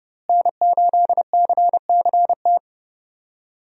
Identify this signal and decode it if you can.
Morse code, heard as a keyed tone, about 31 wpm, 700 Hz. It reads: N8CCT
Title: USA Call Signs - 5 Characters - Repeated 3x - 30wpm